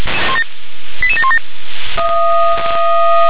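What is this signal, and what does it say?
Five-tone selective call (ZVEI), heard as quick runs of five tones, tones of about 70 ms. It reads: 88717
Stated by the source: ZVEI five-tone selective call 88704 received via SDR.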